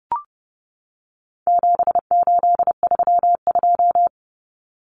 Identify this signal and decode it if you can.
Morse code, heard as a keyed tone, about 30 wpm, 700 Hz. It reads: 7832
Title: Common Cut Numbers - 4-Digit 30wpm